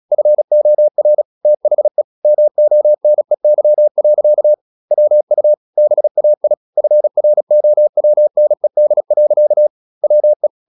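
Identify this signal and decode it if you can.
Morse code, heard as a keyed tone, about 36 wpm, 600 Hz. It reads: FOR THE MONEY. WU BAI FROWNED. WE
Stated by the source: Morse practice at 36 wpm, text of Ken Follett's novel Never, from CW med Gnister.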